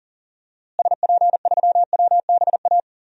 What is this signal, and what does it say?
Morse code, heard as a keyed tone, about 40 wpm, 700 Hz. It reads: SP3WBA
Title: DXCC-100 Call Signs - 6 Characters - Repeated 2x - 40wpm